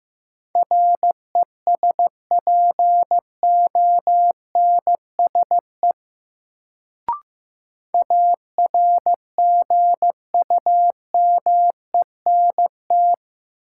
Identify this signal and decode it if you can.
Morse code, heard as a keyed tone, about 15 wpm, 700 Hz. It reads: RESPONSE ARGUMENT
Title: Sentences from Top 1400 Words - New Word Drill - 15wpm